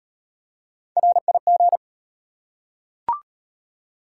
Morse code, heard as a keyed tone, about 35 wpm, 700 Hz. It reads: RIG